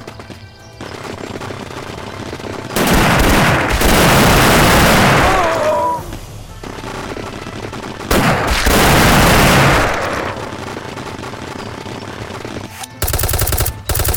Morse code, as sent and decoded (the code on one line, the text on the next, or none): none